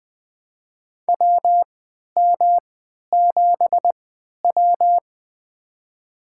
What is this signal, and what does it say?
Morse code, heard as a keyed tone, about 20 wpm, spaced as for 11 wpm, 700 Hz. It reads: WM7W
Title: USA Call Signs - 3x Character Spacing - 20wpm